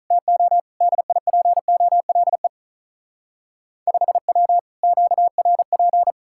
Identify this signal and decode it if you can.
Morse code, heard as a keyed tone, about 45 wpm, 700 Hz. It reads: TO DIPOLE 5W QRP